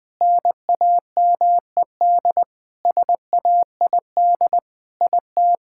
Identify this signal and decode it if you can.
Morse code, heard as a keyed tone, about 20 wpm, 700 Hz. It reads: NAMED SAID IT
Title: Sentences with 7-Word Phrases - 20wpm